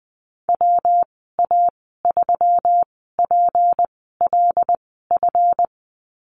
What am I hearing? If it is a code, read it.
Morse code, heard as a keyed tone, about 20 wpm, spaced as for 14 wpm, 700 Hz. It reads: WA3PLF
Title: 179 Lesson 29g - Slash - Call Signs 20wpm